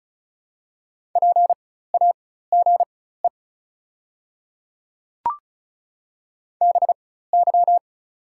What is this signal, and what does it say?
Morse code, heard as a keyed tone, about 34 wpm, spaced as for 16 wpm, 700 Hz. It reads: PAGE BY
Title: 4x Character Spacing - Top 300 Words - 35wpm